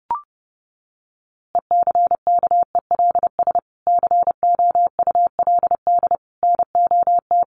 Morse code, heard as a keyed tone, about 30 wpm, 700 Hz. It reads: ECKELS COULD NOT
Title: A Sound Of Thunder 30wpm